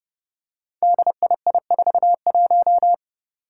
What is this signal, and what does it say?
Morse code, heard as a keyed tone, about 29 wpm, 700 Hz. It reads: DII41
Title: Five Challenging Letter-Number Combinations 30wpm